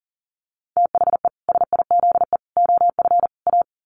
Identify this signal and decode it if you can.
Morse code, heard as a keyed tone, about 37 wpm, 700 Hz. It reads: THE SIZE OF A
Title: Sets of 4 Words 40wpm